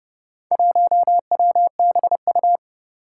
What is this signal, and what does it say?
Morse code, heard as a keyed tone, about 30 wpm, 700 Hz. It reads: 1WBU
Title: Four Challenging Letter-Number Combinations 30wpm